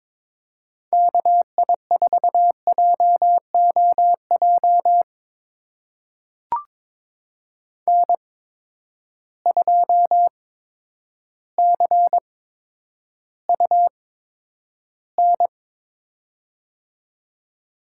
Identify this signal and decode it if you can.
Morse code, heard as a keyed tone, about 22 wpm, spaced as for 10 wpm, 700 Hz. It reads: KI4JOJ N2CUN
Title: USA Call Signs - 8x Character Spacing - 22wpm